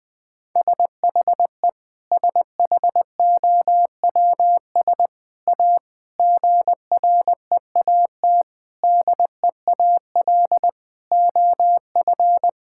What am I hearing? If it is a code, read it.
Morse code, heard as a keyed tone, about 20 wpm, 700 Hz. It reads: SHE SHOWS A GREAT DEAL OF
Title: Sentences with 4-Word Phrases - 20wpm